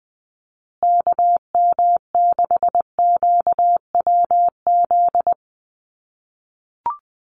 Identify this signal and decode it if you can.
Morse code, heard as a keyed tone, about 20 wpm, 700 Hz. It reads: KM6QWZ
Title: USA Call Signs Repeated 2x 20wpm